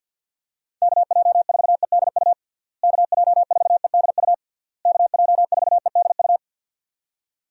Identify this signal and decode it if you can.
Morse code, heard as a keyed tone, about 52 wpm, 700 Hz. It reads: KJ4EDU KJ4EDU KJ4EDU
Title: USA Call Signs - 6 Characters - Repeated 3x - 50wpm